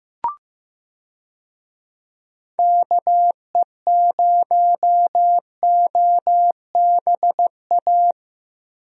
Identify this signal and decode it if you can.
Morse code, heard as a keyed tone, about 15 wpm, 700 Hz. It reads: KE0OBA
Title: USA Call Signs - 6 Characters - Repeated 2x - 15wpm